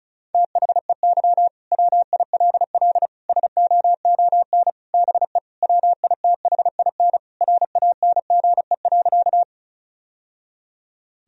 Morse code, heard as a keyed tone, about 36 wpm, 700 Hz. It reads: THEY WILL SOON BE WITHIN RANGE.
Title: Sentences from Top 1000 Words - Review Part 4 - 35wpm